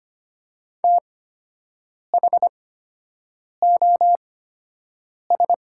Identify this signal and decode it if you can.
Morse code, heard as a keyed tone, about 25 wpm, spaced as for 7 wpm, 690 Hz. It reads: THOS
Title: 8x Character Spacing - Top 300 Words - 25wpm